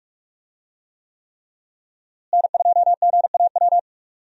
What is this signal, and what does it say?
Morse code, heard as a keyed tone, about 46 wpm, 700 Hz. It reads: N2GAW
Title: USA Call Signs - 5 Characters - Repeated 3x - 45wpm